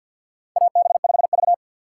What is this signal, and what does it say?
Morse code, heard as a keyed tone, about 52 wpm, 700 Hz. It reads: AB5V
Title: USA Call Signs - 4 Characters - 2x Character Spacing - 50wpm